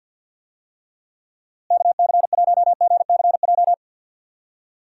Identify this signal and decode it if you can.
Morse code, heard as a keyed tone, about 52 wpm, 700 Hz. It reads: KC1GCJ
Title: USA Call Signs - 6 Characters - Repeated 3x - 50wpm